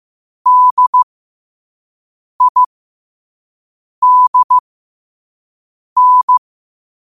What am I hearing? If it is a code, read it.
Morse code, heard as a keyed tone, about 15 wpm, spaced as for 5 wpm, 1000 Hz. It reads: DIDN
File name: qotdfiles / 5wpm